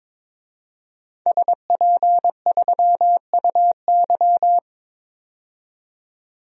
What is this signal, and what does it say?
Morse code, heard as a keyed tone, about 22 wpm, 700 Hz. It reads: SP3UY
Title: DXCC-100 Call Signs - 5 Characters - 22wpm